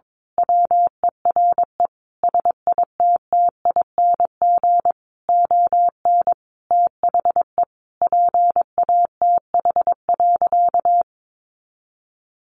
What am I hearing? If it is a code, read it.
Morse code, heard as a keyed tone, about 22 wpm, 700 Hz. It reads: WERE SITTING ON THE PATH.
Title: A Sound Of Thunder 22wpm